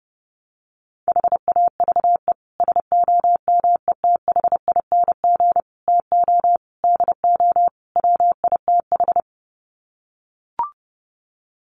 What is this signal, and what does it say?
Morse code, heard as a keyed tone, about 30 wpm, 700 Hz. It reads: HAVE SOMETHING TO DO WITH
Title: Sets of 5 Words Encore 30wpm